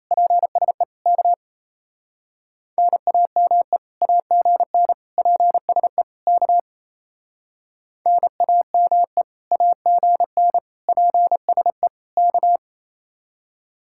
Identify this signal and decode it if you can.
Morse code, heard as a keyed tone, about 35 wpm, 700 Hz. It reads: PSE K NAME AGN PSE K NAME AGN PSE K